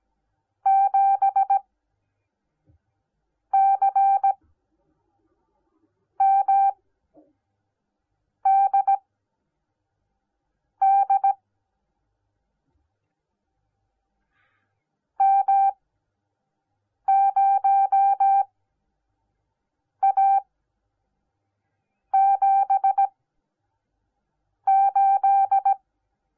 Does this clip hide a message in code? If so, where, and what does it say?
Yes: Morse code from 0.7 to 25.7 s, heard as a keyed tone, about 17 wpm, spaced as for 5 wpm, 780 Hz. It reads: 7CMDD M0A78